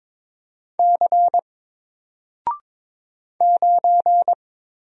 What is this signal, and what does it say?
Morse code, heard as a keyed tone, about 22 wpm, 700 Hz. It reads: C 9
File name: Single-Letter-Number-22wpm